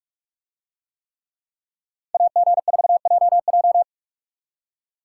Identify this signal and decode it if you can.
Morse code, heard as a keyed tone, about 43 wpm, 700 Hz. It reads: AG4JJ